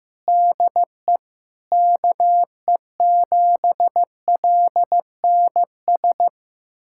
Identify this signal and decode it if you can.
Morse code, heard as a keyed tone, about 15 wpm, 700 Hz. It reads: DE KE7LNS
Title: USA Callsign DE Callsign 15wpm